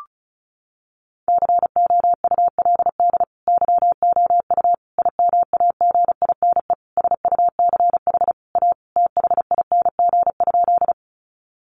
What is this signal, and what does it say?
Morse code, heard as a keyed tone, about 34 wpm, 700 Hz. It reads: COULD YOU IMAGINE SUCH A THING?
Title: Sentences from Top 1400 Words 35wpm